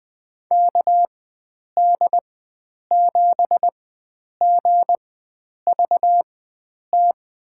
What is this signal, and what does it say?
Morse code, heard as a keyed tone, about 20 wpm, spaced as for 9 wpm, 700 Hz. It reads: KD7GVT
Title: USA Call Signs - 4x Character Spacing - 20wpm